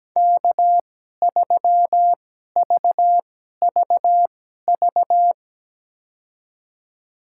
Morse code, heard as a keyed tone, about 17 wpm, spaced as for 12 wpm, 700 Hz. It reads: K3VVV
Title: USA Call Signs - 5 Characters - 2x Character Spacing - 17wpm